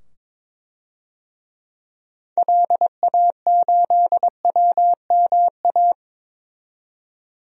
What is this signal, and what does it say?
Morse code, heard as a keyed tone, about 22 wpm, 700 Hz. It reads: LA8WMA
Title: DXCC-100 Call Signs - 6 Characters - 22wpm